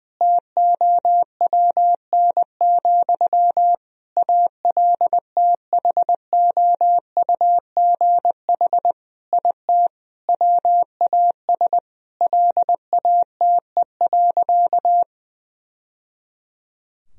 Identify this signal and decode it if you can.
Morse code, heard as a keyed tone, about 20 wpm, 700 Hz. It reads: TOWN, ALTHOUGH IT WAS LATE.
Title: Sentences from Top 1000 Words - Review Part 3 - 20wpm